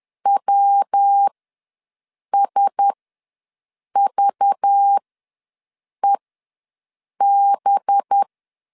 Morse code, heard as a keyed tone, about 10 wpm, spaced as for 6 wpm, 800 Hz. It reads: WSVEB
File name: LEZIONI / 5 LBVC / 5 LBV ws 10wpm 3645